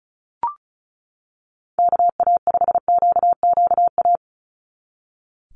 Morse code, heard as a keyed tone, about 36 wpm, 700 Hz. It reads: KA5QQA
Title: NA SOTA Chaser Callsigns 35wpm